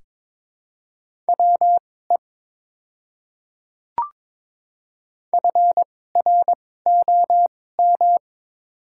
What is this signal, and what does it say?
Morse code, heard as a keyed tone, about 22 wpm, spaced as for 16 wpm, 700 Hz. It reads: WE FROM
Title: 2x Character Spacing - Top 100 Words - 22wpm